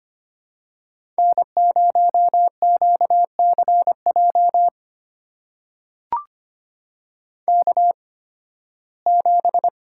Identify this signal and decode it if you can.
Morse code, heard as a keyed tone, about 25 wpm, spaced as for 16 wpm, 700 Hz. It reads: N0QCJ K7
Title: USA Call Signs - 5 Characters - 8x Character Spacing - 25wpm